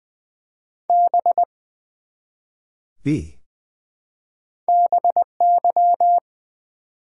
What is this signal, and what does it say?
Morse code, heard as a keyed tone, about 20 wpm, 700 Hz. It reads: B BY
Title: Build A Word - 1 Letter At A Time - 3-Letter Words - 20wpm